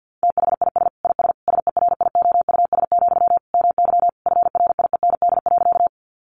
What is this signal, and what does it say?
Morse code, heard as a keyed tone, about 50 wpm, 700 Hz. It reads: THIS IS SERIOUS, MY FRIEND.
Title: Sentences from Top 1100 Words 50wpm